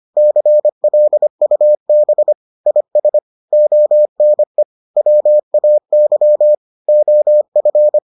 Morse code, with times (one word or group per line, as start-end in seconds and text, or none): 0.2-2.3 CLUB
2.7-3.2 IS
3.5-4.6 ONE
5.0-6.5 WAY
6.9-8.0 OF